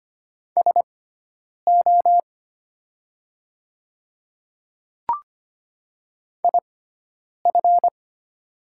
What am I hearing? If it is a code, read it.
Morse code, heard as a keyed tone, about 25 wpm, spaced as for 9 wpm, 700 Hz. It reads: SO IF
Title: 6x Character Spacing - Top 200 Words - 25wpm